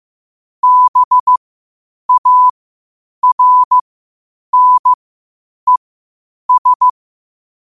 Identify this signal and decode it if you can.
Morse code, heard as a keyed tone, about 15 wpm, spaced as for 8 wpm, 1000 Hz. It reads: BARNES
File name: qotdfiles / 8wpm